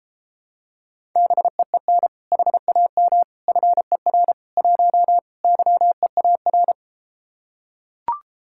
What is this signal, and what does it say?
Morse code, heard as a keyed tone, about 33 wpm, 700 Hz. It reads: BEEN HAM FER 1 YEAR